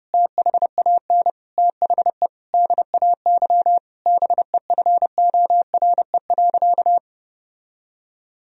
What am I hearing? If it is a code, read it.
Morse code, heard as a keyed tone, about 32 wpm, 700 Hz. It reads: THAN THE DAY BEFORE.
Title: Sentences from Top 1000 Words - Review Part 1 - 30wpm